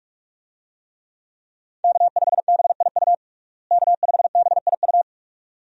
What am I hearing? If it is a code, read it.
Morse code, heard as a keyed tone, about 42 wpm, 700 Hz. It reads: K5BIU K5BIU
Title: USA Call Signs - 5 Characters - Repeated 2x - 45wpm